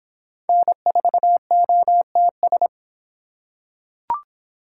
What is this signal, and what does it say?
Morse code, heard as a keyed tone, about 24 wpm, 700 Hz. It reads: N4OTS